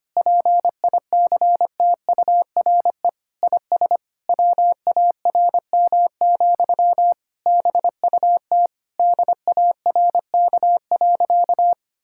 Morse code, heard as a keyed tone, about 25 wpm, 700 Hz. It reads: PICTURE IS WARM, BUT DARK.
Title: Sentences from Top 1600 Words 25wpm